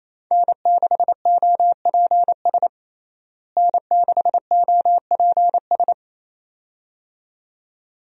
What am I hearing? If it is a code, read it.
Morse code, heard as a keyed tone, about 28 wpm, 700 Hz. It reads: N6OPS N6OPS
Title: USA Call Signs - 5 Characters - Repeated 2x - 28wpm